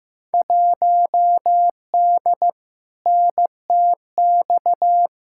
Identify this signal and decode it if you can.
Morse code, heard as a keyed tone, about 15 wpm, 700 Hz. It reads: 1D NTX
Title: Field Day Exchanges - 15wpm